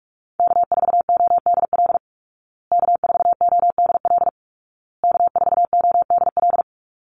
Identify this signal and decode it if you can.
Morse code, heard as a keyed tone, about 44 wpm, 700 Hz. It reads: K4ODL K4ODL K4ODL
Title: Call Signs Repeated 2x - 45wpm